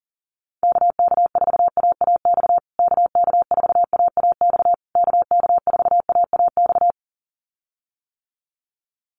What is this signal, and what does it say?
Morse code, heard as a keyed tone, about 37 wpm, 700 Hz. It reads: KK4AAX KK4AAX KK4AAX
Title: USA Call Signs Repeated 3x 40wpm